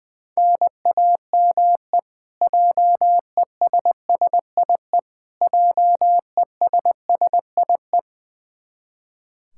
Morse code, heard as a keyed tone, about 20 wpm, 700 Hz. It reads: NAME JESSIE JESSIE
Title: Practice QSO - Exchange 1 - Beginning - 20wpm